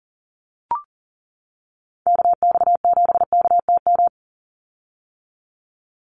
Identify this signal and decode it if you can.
Morse code, heard as a keyed tone, about 40 wpm, 700 Hz. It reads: KX7KTM
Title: USA Call Signs 40wpm